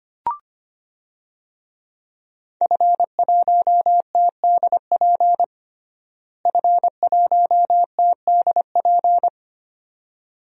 Morse code, heard as a keyed tone, about 25 wpm, 700 Hz. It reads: F1TDP F1TDP
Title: DXCC-100 Call Signs - 5 Characters - Repeated 2x - 25wpm